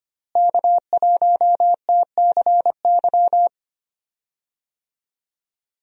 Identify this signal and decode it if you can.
Morse code, heard as a keyed tone, about 25 wpm, 700 Hz. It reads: K1TCY
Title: USA Call Signs - 5 Characters - Repeated 3x - 25wpm